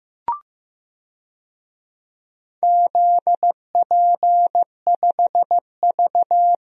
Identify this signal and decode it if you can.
Morse code, heard as a keyed tone, about 15 wpm, 700 Hz. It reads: ZP5V
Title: DXCC-100 Call Signs - 4 Characters - 15wpm